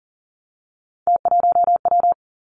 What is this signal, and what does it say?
Morse code, heard as a keyed tone, about 38 wpm, 700 Hz. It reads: T1W
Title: Three Challenging Letter-Number Combinations 40wpm